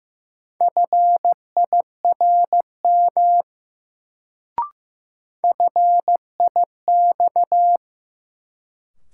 Morse code, heard as a keyed tone, about 15 wpm, 700 Hz. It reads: FIRM FIX
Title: Sentences from Top 1400 Words - New Word Drill Alt - 15wpm